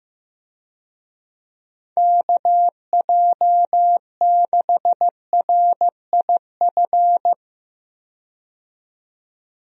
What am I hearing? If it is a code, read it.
Morse code, heard as a keyed tone, about 15 wpm, 700 Hz. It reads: KJ6RIF
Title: USA Call Signs - 6 Characters - 15wpm